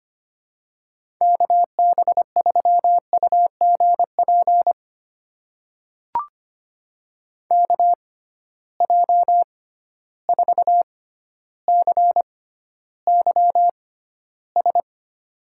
Morse code, heard as a keyed tone, about 25 wpm, spaced as for 13 wpm, 700 Hz. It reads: KB3UGP KJ4CYS
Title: USA Call Signs - 6 Characters - 6x Character Spacing - 25wpm